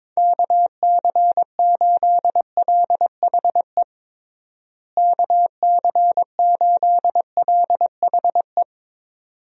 Morse code, heard as a keyed tone, about 22 wpm, 700 Hz. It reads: KC8LHE KC8LHE
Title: USA Call Signs - 6 Characters - Repeated 2x - 22wpm